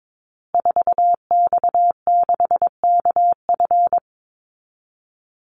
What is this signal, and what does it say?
Morse code, heard as a keyed tone, about 22 wpm, 700 Hz. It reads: 4X6KF
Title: Call Signs Repeated 2x - 22wpm